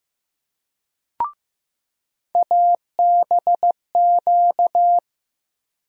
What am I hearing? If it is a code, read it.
Morse code, heard as a keyed tone, about 15 wpm, 700 Hz. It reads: ABQ
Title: USA Call Sign Postfixes - 3 Letters 15wpm